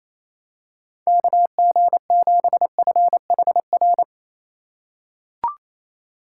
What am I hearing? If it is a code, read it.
Morse code, heard as a keyed tone, about 28 wpm, 700 Hz. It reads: KG7FHR
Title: USA Call Signs - 6 Characters - 2x Character Spacing - 28wpm